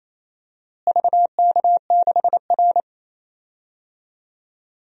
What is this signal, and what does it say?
Morse code, heard as a keyed tone, about 28 wpm, 700 Hz. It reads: VK6R
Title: DXCC-100 Call Signs - 4 Characters - Repeated 2x - 28wpm